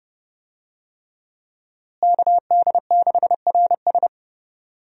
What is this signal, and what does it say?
Morse code, heard as a keyed tone, about 30 wpm, 700 Hz. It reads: KD6RS
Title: USA Call Signs - 5 Characters - Repeated 3x - 30wpm